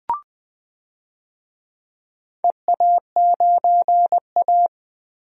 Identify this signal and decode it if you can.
Morse code, heard as a keyed tone, about 20 wpm, 700 Hz. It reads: EA9A
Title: DXCC-100 Call Signs - 4 Characters - Repeated 2x - 20wpm